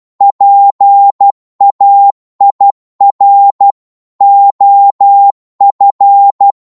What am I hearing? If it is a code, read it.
Morse code, heard as a keyed tone, about 12 wpm, 800 Hz. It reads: PAIR OF